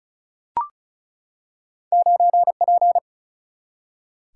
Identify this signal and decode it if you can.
Morse code, heard as a keyed tone, about 36 wpm, 700 Hz. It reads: 9 P
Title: Single-Letter Single-Number 35wpm